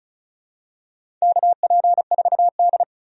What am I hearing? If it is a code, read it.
Morse code, heard as a keyed tone, about 36 wpm, 690 Hz. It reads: KP4D
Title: DXCC-100 Call Signs - 4 Characters - 35wpm